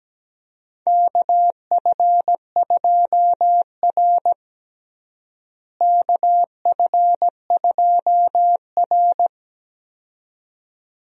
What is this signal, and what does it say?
Morse code, heard as a keyed tone, about 17 wpm, 700 Hz. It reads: KF2R KF2R
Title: USA Call Signs - 4 Characters - Repeated 2x - 17wpm